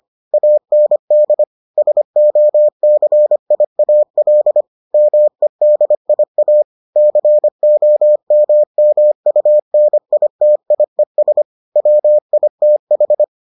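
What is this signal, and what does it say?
Morse code, heard as a keyed tone, about 25 wpm, 600 Hz. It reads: AND SOCIAL MEDIA COMMUNITIES WITH
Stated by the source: Morse edition of the Foundations of Amateur Radio episodes.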